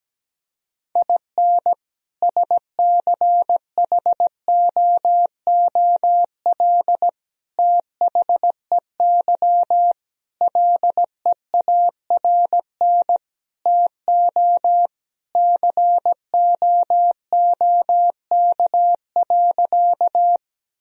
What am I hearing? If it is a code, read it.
Morse code, heard as a keyed tone, about 17 wpm, 700 Hz. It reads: IN SCHOOL THEY LEARN TO COOK.
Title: Sentences from Top 1000 Words - Review Part 4 - 17wpm